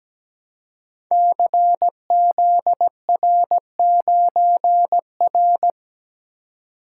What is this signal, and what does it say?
Morse code, heard as a keyed tone, about 17 wpm, 690 Hz. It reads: CZR9R
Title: Code Groups - 5 Letter:Numbers - 17wpm